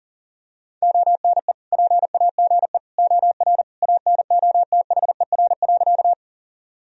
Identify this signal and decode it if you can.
Morse code, heard as a keyed tone, about 40 wpm, 700 Hz. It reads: ONE PAGE OR ANOTHER.
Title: Sentences from Top 1000 Words - Review Part 1 - 40wpm